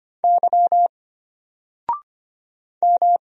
Morse code, heard as a keyed tone, about 25 wpm, 700 Hz. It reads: Y M
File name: Single-Letter-Number-25wpm